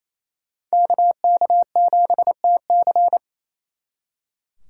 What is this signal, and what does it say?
Morse code, heard as a keyed tone, about 35 wpm, 700 Hz. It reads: KK7TC